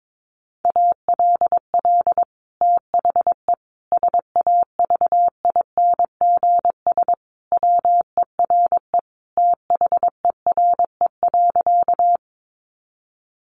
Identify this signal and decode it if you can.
Morse code, heard as a keyed tone, about 22 wpm, 700 Hz. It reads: ALL THE SAVINGS WERE THERE.
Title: Sentences from Top 1200 Words 22wpm